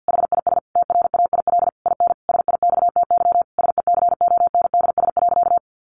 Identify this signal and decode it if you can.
Morse code, heard as a keyed tone, about 50 wpm, 700 Hz. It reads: HIS TRAIL IN SIXTY SECONDS.
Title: A Sound Of Thunder 50wpm